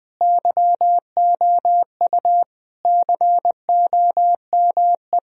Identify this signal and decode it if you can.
Morse code, heard as a keyed tone, about 20 wpm, 700 Hz. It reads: YOU COME
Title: Sentences from Top 1000 Words - Review Part 2 - 20wpm